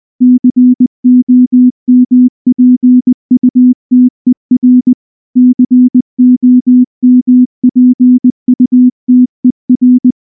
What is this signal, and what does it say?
Morse code, heard as a keyed tone, about 19 wpm, 250 Hz. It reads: COMPUTER COMPUTER
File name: morse / morse-training-20WPM-10